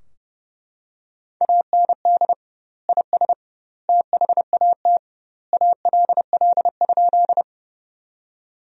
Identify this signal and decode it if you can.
Morse code, heard as a keyed tone, about 32 wpm, 700 Hz. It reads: AND IS THAT ALL?
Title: 2x Word-Spacing - Sentences from Top 100 Words - 30wpm